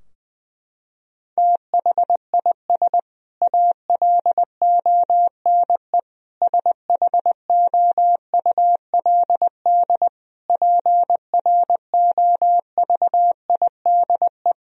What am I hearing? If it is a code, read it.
Morse code, heard as a keyed tone, about 20 wpm, 700 Hz. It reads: THIS ALONE SHOULD PROVIDE
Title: Sentences from Top 1000 Words - Review Part 4 - 20wpm